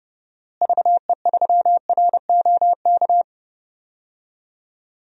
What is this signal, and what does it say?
Morse code, heard as a keyed tone, about 30 wpm, 700 Hz. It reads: VE3ROK
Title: DXCC-100 Call Signs - 6 Characters - Repeated 2x - 30wpm